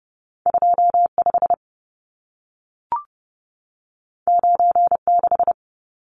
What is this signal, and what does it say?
Morse code, heard as a keyed tone, about 30 wpm, 700 Hz. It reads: 25 96
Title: Numbers - 2-Digit 30wpm